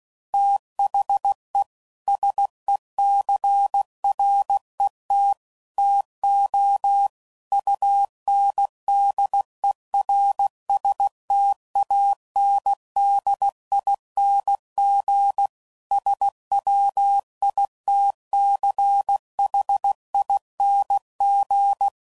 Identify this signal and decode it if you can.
Morse code, heard as a keyed tone, about 16 wpm, 800 Hz. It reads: THE SECRET TO UNDERSTANDING SWITCHING
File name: CW Code Groups @15WPM / Technical Subjects / PwrSupply